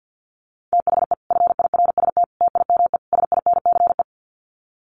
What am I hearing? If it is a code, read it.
Morse code, heard as a keyed tone, about 50 wpm, 700 Hz. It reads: THE FIRST TIME SINCE
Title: Sets of 4 Words 50wpm